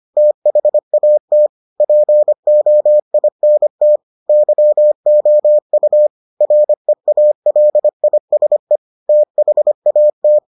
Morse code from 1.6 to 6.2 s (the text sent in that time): POINT YOU